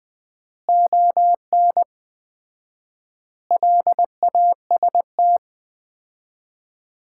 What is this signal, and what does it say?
Morse code, heard as a keyed tone, about 20 wpm, 700 Hz. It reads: ON LAST
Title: Common QSO Phrases - 4x Word-Spacing - 20wpm